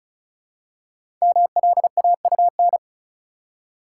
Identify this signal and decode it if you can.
Morse code, heard as a keyed tone, about 36 wpm, 700 Hz. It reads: MLAUN
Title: Code Groups - 5 Letters - 35wpm